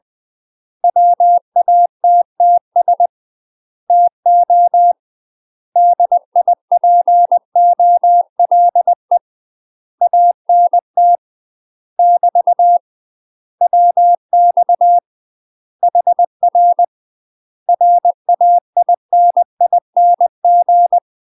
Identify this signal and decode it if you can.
Morse code, heard as a keyed tone, about 20 wpm, 700 Hz. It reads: WATTS TO DIPOLE ANT <BT> WX HR RAINING
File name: UK MASTER QSOs TRAINING / 20WPM / UK MASTER QSOs 0038